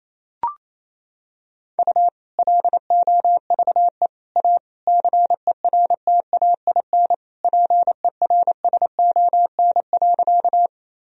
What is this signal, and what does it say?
Morse code, heard as a keyed tone, about 28 wpm, 700 Hz. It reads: U LOVE A CERTAIN PERSON.
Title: Sentences from Top 500 Words and Abbreviations 28wpm